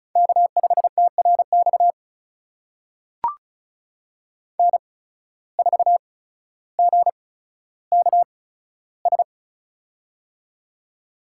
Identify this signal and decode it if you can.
Morse code, heard as a keyed tone, about 35 wpm, spaced as for 15 wpm, 690 Hz. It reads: K5TRX N4GKS
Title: USA Call Signs - 5 Characters - 8x Character Spacing - 35wpm